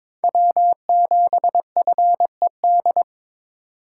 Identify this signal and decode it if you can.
Morse code, heard as a keyed tone, about 26 wpm, 700 Hz. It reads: W7FED
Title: Speed Racing - USA 1x3 Call Signs 22wpm